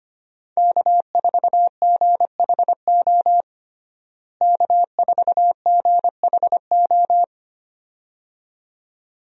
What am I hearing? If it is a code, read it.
Morse code, heard as a keyed tone, about 25 wpm, 700 Hz. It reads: K4GHO K4GHO
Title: USA Call Signs - 5 Characters - Repeated 2x - 25wpm